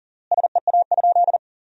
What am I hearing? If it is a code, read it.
Morse code, heard as a keyed tone, about 40 wpm, 700 Hz. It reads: SEA?
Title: 3x Word-Spacing - Sentences from Top 300 Words - 40wpm